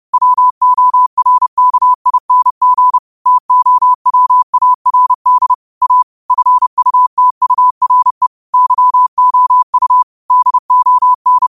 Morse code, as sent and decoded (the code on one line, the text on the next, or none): .-- --- .-. -.- .. -. --. / - --- .-- .- .-. -.. / .- / ..-. ..- - ..- .-. / -.-- --- ..- / -.. --- -.
WORKING TOWARD A FUTURE YOU DON